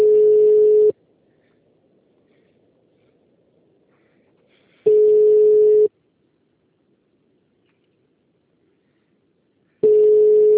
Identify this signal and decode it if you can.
Ringback tone (Europe, 425 Hz).
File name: out- -2002- - - .1496